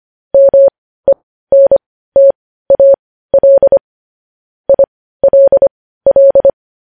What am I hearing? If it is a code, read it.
Morse code, heard as a keyed tone, about 25 wpm, spaced as for 15 wpm, 550 Hz. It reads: MENTAL ILL
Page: news headlines in morse code at 15 wpm